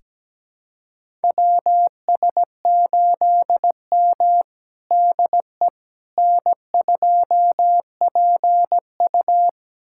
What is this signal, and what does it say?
Morse code, heard as a keyed tone, about 17 wpm, 700 Hz. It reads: WS8M DE N2PU
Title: USA Callsign DE Callsign - 4 Characters - 17wpm